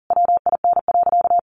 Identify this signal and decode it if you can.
Morse code, heard as a keyed tone, about 40 wpm, 700 Hz. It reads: WIN.
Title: Sentences from Top 1500 Words 40wpm